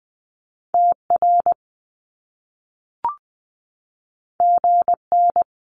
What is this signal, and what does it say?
Morse code, heard as a keyed tone, about 20 wpm, 700 Hz. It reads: TR GN
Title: Top Two Letter Combinations in Words 20wpm